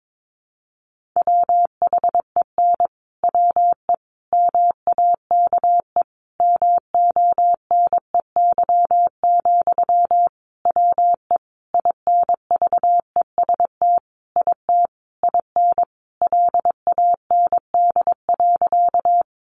Morse code, heard as a keyed tone, about 22 wpm, 700 Hz. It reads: WHEN WE MAKE MONEY, WE INVEST IT IN LAND.
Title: Sentences from Top 1100 Words 22wpm